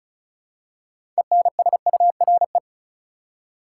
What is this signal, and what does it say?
Morse code, heard as a keyed tone, about 34 wpm, 700 Hz. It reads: ENSURE
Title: Six Letter Words - 35wpm